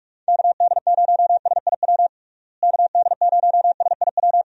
Morse code, heard as a keyed tone, about 41 wpm, 700 Hz. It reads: KD0SIW KD0SIW
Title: USA Call Signs - 6 Characters - Repeated 2x - 45wpm